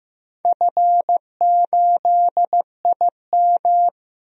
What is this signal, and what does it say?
Morse code, heard as a keyed tone, about 16 wpm, 700 Hz. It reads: F8IM
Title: DXCC-100 Call Signs - 4 Characters - Repeated 3x - 15wpm